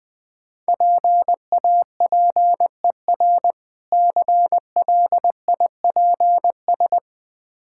Morse code, heard as a keyed tone, about 20 wpm, 700 Hz. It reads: PAPER CLIPS
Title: Difficult Words with Hints - 20wpm